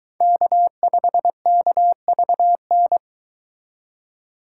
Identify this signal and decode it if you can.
Morse code, heard as a keyed tone, about 23 wpm, 700 Hz. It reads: K5KVN